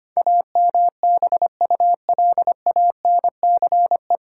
Morse code, heard as a keyed tone, about 25 wpm, 700 Hz. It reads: AMBULANCE
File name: practice / 25wpm / 9 Letter Words - First 3 Letters Given - 25wpm